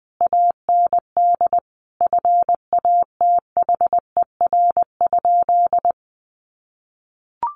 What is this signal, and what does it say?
Morse code, heard as a keyed tone, about 19 wpm, 700 Hz. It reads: AND FATHER?
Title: Sentences from Top 1200 Words 20wpm